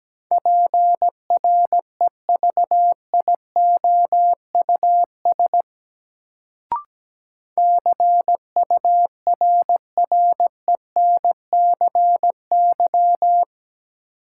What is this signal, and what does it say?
Morse code, heard as a keyed tone, about 17 wpm, 700 Hz. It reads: PREVIOUS CURRENCY
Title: Sentences from Top 1500 Words - New Word Drill - 17wpm